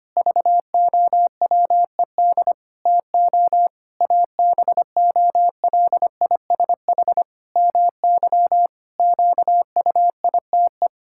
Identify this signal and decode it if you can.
Morse code, heard as a keyed tone, about 25 wpm, 700 Hz. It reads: VOWED TO ABOLISH MY QUITE
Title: Pangram Challenge v2 - 25wpm